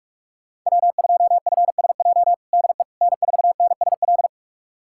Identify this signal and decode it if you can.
Morse code, heard as a keyed tone, about 44 wpm, 700 Hz. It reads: W2FSJ DE N4NSL
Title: USA Callsign DE Callsign - 5 Characters - 45wpm